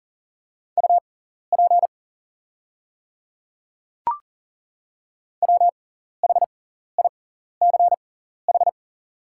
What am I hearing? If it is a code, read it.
Morse code, heard as a keyed tone, about 40 wpm, spaced as for 14 wpm, 700 Hz. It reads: UP WHICH
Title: 6x Character Spacing - Top 100 Words - 40wpm